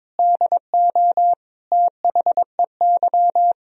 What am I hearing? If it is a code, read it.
Morse code, heard as a keyed tone, about 22 wpm, 700 Hz. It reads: DO THEY
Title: Sentences from Top 1000 Words - Review Part 3 - 22wpm